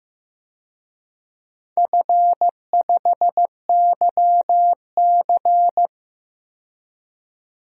Morse code, heard as a keyed tone, about 15 wpm, 700 Hz. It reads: F5YC
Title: DXCC-100 Call Signs - 4 Characters - 15wpm